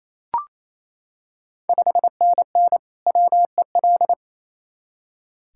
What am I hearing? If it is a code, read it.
Morse code, heard as a keyed tone, about 27 wpm, 690 Hz. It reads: 5NN WEL
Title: Ontario QSO Party Exchange 28wpm